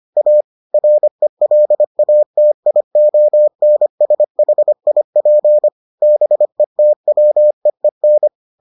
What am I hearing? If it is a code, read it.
Morse code, heard as a keyed tone, about 25 wpm, 600 Hz. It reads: A RELATIONSHIP BETWEEN